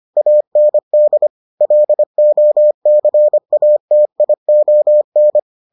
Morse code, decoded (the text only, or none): AND LOCATION